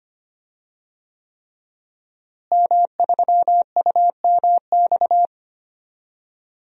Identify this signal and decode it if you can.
Morse code, heard as a keyed tone, about 25 wpm, 700 Hz. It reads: M3UMX
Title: DXCC-100 Call Signs - 5 Characters - Repeated 3x - 25wpm